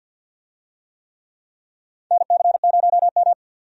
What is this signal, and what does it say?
Morse code, heard as a keyed tone, about 51 wpm, 700 Hz. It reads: NC0M
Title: USA Call Signs - 4 Characters - 50wpm